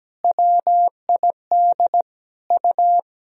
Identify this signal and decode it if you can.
Morse code, heard as a keyed tone, about 17 wpm, 700 Hz. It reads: WID U
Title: Sentences from Top 300 Words and Abbreviations 17wpm